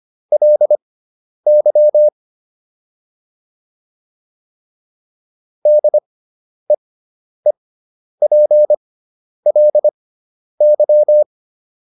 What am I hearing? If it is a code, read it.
Morse code, heard as a keyed tone, about 25 wpm, spaced as for 10 wpm, 600 Hz. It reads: LY DEEPLY